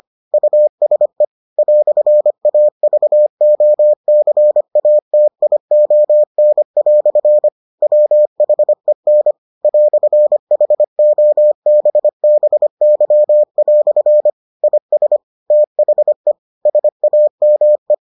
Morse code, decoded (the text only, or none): USE "AVOCATION" WHEN "HOBBY" IS THE SAME